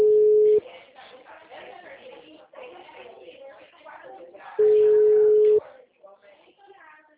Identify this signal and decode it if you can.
Ringback tone (Europe, 425 Hz), with speech.